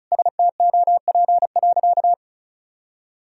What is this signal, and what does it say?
Morse code, heard as a keyed tone, about 36 wpm, 700 Hz. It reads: STOP.
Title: Sentences from Top 1000 Words - Review Part 1 - 35wpm